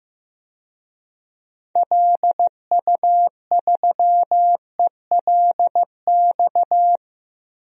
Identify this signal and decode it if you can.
Morse code, heard as a keyed tone, about 15 wpm, 700 Hz. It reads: LU3ELX
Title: DXCC-100 Call Signs - 6 Characters - Repeated 3x - 15wpm